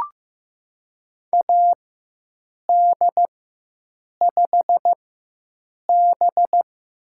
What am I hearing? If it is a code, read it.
Morse code, heard as a keyed tone, about 15 wpm, spaced as for 7 wpm, 700 Hz. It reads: AD5B
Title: USA Call Signs - 4 Characters - 4x Character Spacing - 15wpm